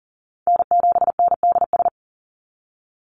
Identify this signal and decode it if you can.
Morse code, heard as a keyed tone, about 40 wpm, 690 Hz. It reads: N7NDS
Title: USA Call Signs 40wpm